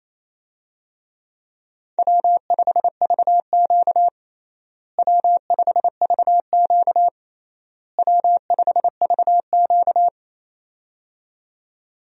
Morse code, heard as a keyed tone, about 28 wpm, 700 Hz. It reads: W5VQ W5VQ W5VQ
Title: USA Call Signs - 4 Characters - Repeated 3x - 28wpm